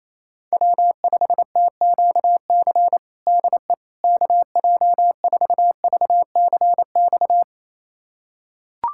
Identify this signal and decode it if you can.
Morse code, heard as a keyed tone, about 28 wpm, 700 Hz. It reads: W5TQC DE KJ4VCX
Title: USA Callsign DE Callsign 28wpm